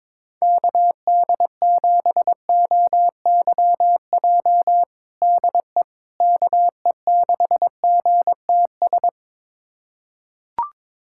Morse code, heard as a keyed tone, about 21 wpm, 700 Hz. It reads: KD7OYJ DE KE6GTS
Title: USA Callsign DE Callsign - 6 Characters - 22wpm